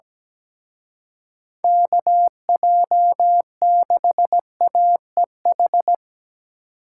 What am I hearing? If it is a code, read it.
Morse code, heard as a keyed tone, about 20 wpm, 700 Hz. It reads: KJ6AEH
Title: Speed Racing - USA 2x3 Call Signs 17wpm